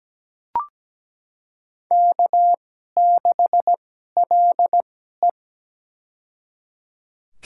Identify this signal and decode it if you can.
Morse code, heard as a keyed tone, about 17 wpm, spaced as for 12 wpm, 700 Hz. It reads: K6LE
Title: USA Call Signs - 2x Character Spacing - 17wpm